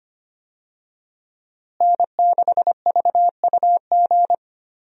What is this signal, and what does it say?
Morse code, heard as a keyed tone, about 25 wpm, 700 Hz. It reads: N6VUG
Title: USA Call Signs - 5 Characters - Repeated 2x - 25wpm